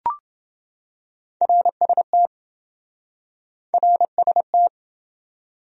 Morse code, heard as a keyed tone, about 27 wpm, 700 Hz. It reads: RST RST